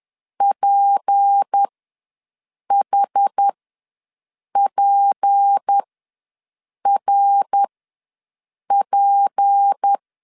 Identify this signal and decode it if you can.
Morse code, heard as a keyed tone, about 10 wpm, spaced as for 6 wpm, 800 Hz. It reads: PHPRP
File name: LEZIONI / 6 PKX0 / 6 P ws 10wpm 3751